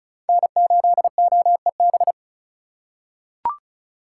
Morse code, heard as a keyed tone, about 36 wpm, 700 Hz. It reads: N8OEB
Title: USA Call Signs - 5 Characters - 2x Character Spacing - 35wpm